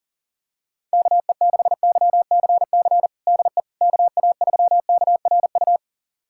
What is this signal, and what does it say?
Morse code, heard as a keyed tone, about 40 wpm, 700 Hz. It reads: KE6YCC DE KA3KRU
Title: USA Callsign DE Callsign 40wpm